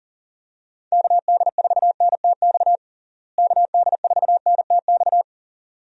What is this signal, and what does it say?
Morse code, heard as a keyed tone, about 40 wpm, 700 Hz. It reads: KD4NTX KD4NTX
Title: USA Call Signs - 6 Characters - Repeated 2x - 40wpm